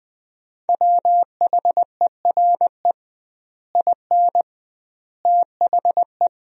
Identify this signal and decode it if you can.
Morse code, heard as a keyed tone, about 20 wpm, 700 Hz. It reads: WHERE IN THE
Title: 2x Word-Spacing - Sentences from Top 200 Words - 20wpm